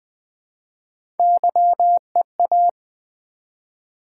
Morse code, heard as a keyed tone, about 20 wpm, 700 Hz. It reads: YEA
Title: Build A Word - 1 Letter At A Time - 3 to 5-Letter Words - 20wpm